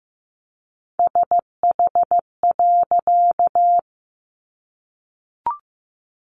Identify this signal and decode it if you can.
Morse code, heard as a keyed tone, about 15 wpm, 700 Hz. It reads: SH.
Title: A Sound Of Thunder 15wpm